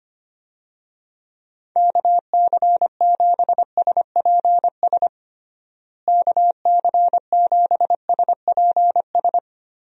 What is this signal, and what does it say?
Morse code, heard as a keyed tone, about 25 wpm, 700 Hz. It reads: KC7SPS KC7SPS
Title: USA Call Signs - 6 Characters - Repeated 3x - 25wpm